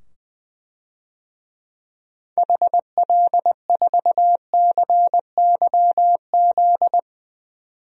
Morse code, heard as a keyed tone, about 20 wpm, 700 Hz. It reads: HL4CYZ
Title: DXCC-100 Call Signs - 6 Characters - 20wpm